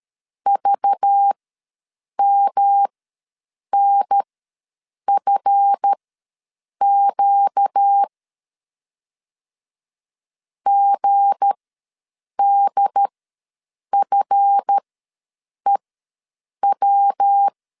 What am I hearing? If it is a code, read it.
Morse code, heard as a keyed tone, about 13 wpm, spaced as for 7 wpm, 800 Hz. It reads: VMNFQ GDFEW
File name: LEZIONI / 7 GF9Z / 7 GF ws 12wpm 3140